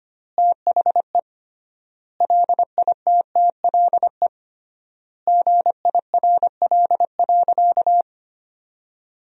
Morse code, heard as a keyed tone, about 25 wpm, 700 Hz. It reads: THE LITTLE GIRL.
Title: 3x Word-Spacing - Sentences from Top 400 Words - 25wpm